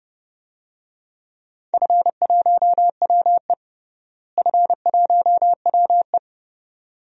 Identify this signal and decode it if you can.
Morse code, heard as a keyed tone, about 30 wpm, 700 Hz. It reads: F1WE F1WE
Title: DXCC-100 Call Signs - 4 Characters - Repeated 2x - 30wpm